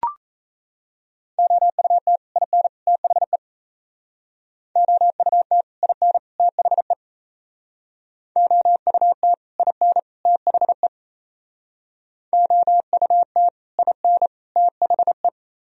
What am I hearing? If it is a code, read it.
Morse code, heard as a keyed tone, about 36 wpm, 700 Hz. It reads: OUT IN THE OUT IN THE OUT IN THE OUT IN THE